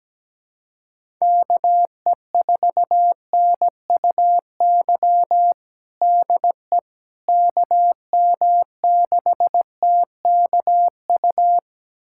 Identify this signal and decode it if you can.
Morse code, heard as a keyed tone, about 17 wpm, 700 Hz. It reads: KE4NUY DE KM6TKU
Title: USA Callsign DE Callsign 17wpm